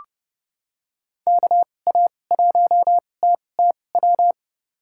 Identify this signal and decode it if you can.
Morse code, heard as a keyed tone, about 30 wpm, spaced as for 22 wpm, 700 Hz. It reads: KA1TTW
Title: USA Call Signs - 6 Characters - 2x Character Spacing - 30wpm